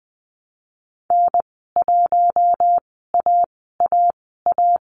Morse code, heard as a keyed tone, about 20 wpm, spaced as for 15 wpm, 700 Hz. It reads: N1AAA
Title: 154 Lesson 25g - Letter p - Call Signs 20wpm